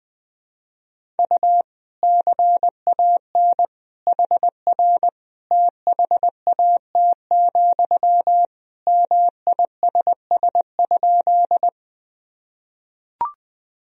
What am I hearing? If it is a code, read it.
Morse code, heard as a keyed tone, about 21 wpm, 700 Hz. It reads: U CAN HR THAT, MISS?
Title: Sentences from Top 500 Words and Abbreviations 20wpm